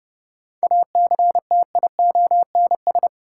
Morse code, heard as a keyed tone, about 30 wpm, 700 Hz. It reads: ACTIONS
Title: Sentences from Top 1000 Words - Review Part 6 - New Word Drill - 30wpm